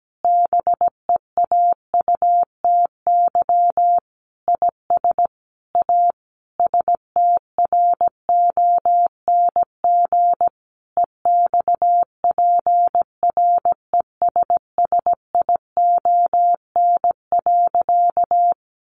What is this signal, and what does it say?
Morse code, heard as a keyed tone, about 17 wpm, 700 Hz. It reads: BEAUTY IS A STRONG EXPRESSION.
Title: Sentences from Top 1200 Words 17wpm